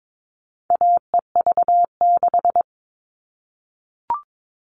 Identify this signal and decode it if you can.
Morse code, heard as a keyed tone, about 22 wpm, 700 Hz. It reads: AEV6
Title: All Cut Numbers - 4-Digit 22wpm